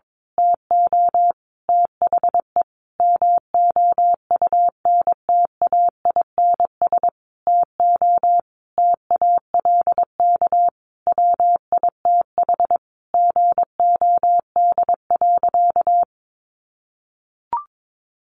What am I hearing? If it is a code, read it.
Morse code, heard as a keyed tone, about 22 wpm, 700 Hz. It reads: TO THE MOUNTAINS TO TALK WITH GOD.
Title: A Sound Of Thunder 22wpm